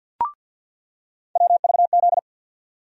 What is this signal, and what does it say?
Morse code, heard as a keyed tone, about 46 wpm, 700 Hz. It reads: WVZ
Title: USA Call Sign Postfixes - 3 Letters 50wpm